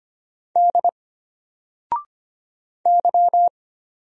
Morse code, heard as a keyed tone, about 25 wpm, 700 Hz. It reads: D Y
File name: Single-Letter-Number-25wpm